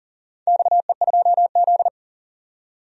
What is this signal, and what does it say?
Morse code, heard as a keyed tone, about 39 wpm, 700 Hz. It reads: XE2Z
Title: DXCC-100 Call Signs - 4 Characters - 40wpm